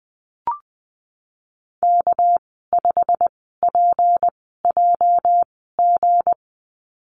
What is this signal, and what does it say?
Morse code, heard as a keyed tone, about 20 wpm, spaced as for 14 wpm, 700 Hz. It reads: K5PJG
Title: 213 Lesson 34g - Number 8 - Call Signs 20wpm